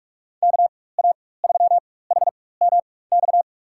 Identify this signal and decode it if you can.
Morse code, heard as a keyed tone, about 46 wpm, spaced as for 21 wpm, 700 Hz. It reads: KA3HMX
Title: USA Call Signs - 4x Character Spacing - 45wpm